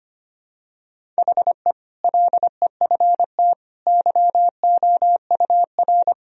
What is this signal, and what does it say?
Morse code, heard as a keyed tone, about 25 wpm, 700 Hz. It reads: HE LEFT YOUR
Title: Sentences from Top 1000 Words - Review Part 1 - 25wpm